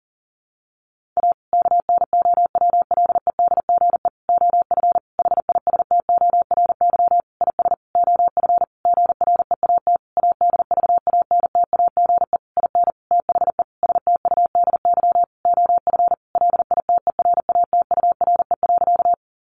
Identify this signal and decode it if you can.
Morse code, heard as a keyed tone, about 40 wpm, 700 Hz. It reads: A KNOWLEDGE OF HISTORY IS OF GREAT ADVANTAGE IN THE STUDY OF LITERATURE.
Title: Sentences from Top 900 Words 40wpm